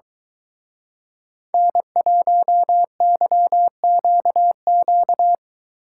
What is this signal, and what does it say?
Morse code, heard as a keyed tone, about 22 wpm, 700 Hz. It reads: N1YQQ